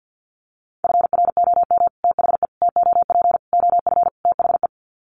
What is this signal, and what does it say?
Morse code, heard as a keyed tone, about 50 wpm, 700 Hz. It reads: FROM THE TOP OF THE
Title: Sets of 5 Words Encore 50wpm